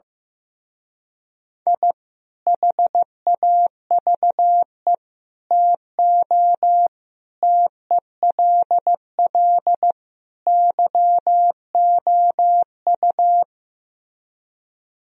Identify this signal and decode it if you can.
Morse code, heard as a keyed tone, about 15 wpm, 700 Hz. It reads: I HAVE TO TELL YOU
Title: Speed-Racing - Sets of 5 Words 15wpm